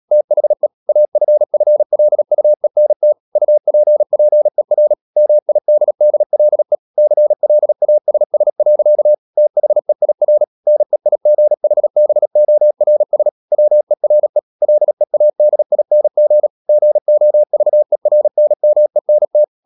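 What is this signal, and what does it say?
Morse code, heard as a keyed tone, about 37 wpm, 600 Hz. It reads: THE AFFLUENT UPPER MIDDLE CLASS. THEIR NEIGHBORS WERE LEADING GOVERNMENT
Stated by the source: Morse code practice at 35–39 wpm sending English text of Ken Follett's novel Never, from CW med Gnister.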